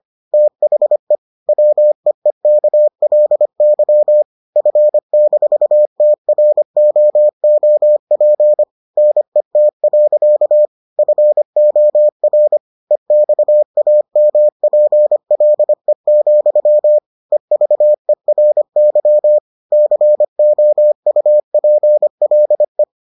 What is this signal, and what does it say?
Morse code, heard as a keyed tone, about 25 wpm, 600 Hz. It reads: THE WEEKLY F-TROOP NET. FOR EXAMPLE, EVERY COUPLE